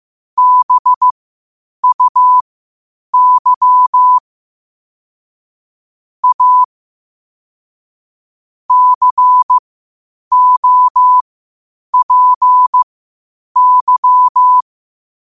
Morse code, heard as a keyed tone, about 15 wpm, spaced as for 8 wpm, 1000 Hz. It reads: BUY A COPY
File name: qotdfiles / 8wpm